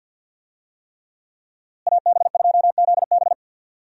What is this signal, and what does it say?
Morse code, heard as a keyed tone, about 53 wpm, 700 Hz. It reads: AB2ZB
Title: USA Call Signs - 5 Characters - Repeated 3x - 50wpm